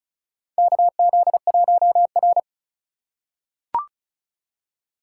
Morse code, heard as a keyed tone, about 36 wpm, 700 Hz. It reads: KZ1R
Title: USA Call Signs - 4x Character Spacing - 35wpm